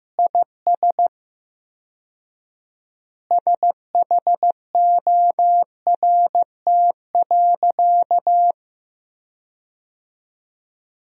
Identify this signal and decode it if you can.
Morse code, heard as a keyed tone, about 15 wpm, 700 Hz. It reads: IS SHORT.
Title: 4x Word-Spacing - Sentences from Top 500 Words - 15wpm